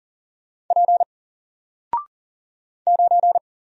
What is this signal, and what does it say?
Morse code, heard as a keyed tone, about 40 wpm, 700 Hz. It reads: P 9
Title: Single Letter-Number Alt 40wpm